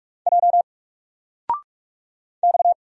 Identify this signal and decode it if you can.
Morse code, heard as a keyed tone, about 46 wpm, 700 Hz. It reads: J X
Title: Single Letter-Number Alt 45wpm